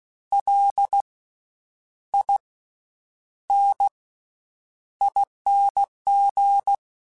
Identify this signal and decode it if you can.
Morse code, heard as a keyed tone, about 16 wpm, spaced as for 8 wpm, 800 Hz. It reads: LINING
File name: Special INGs / Special INGs-mp3s / VerySlowToFaster Ing @15